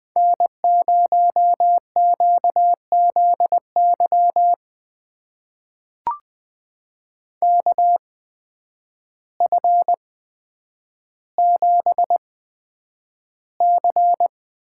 Morse code, heard as a keyed tone, about 20 wpm, spaced as for 9 wpm, 700 Hz. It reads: N0QZY KF7C
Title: USA Call Signs - 5 Characters - 8x Character Spacing - 20wpm